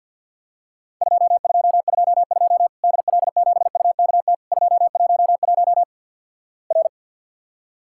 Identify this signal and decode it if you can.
Morse code, heard as a keyed tone, about 49 wpm, 700 Hz. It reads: 2222 DL7UKT 211 R
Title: Worked All Europe DX Contest - QTC Exchange - 50wpm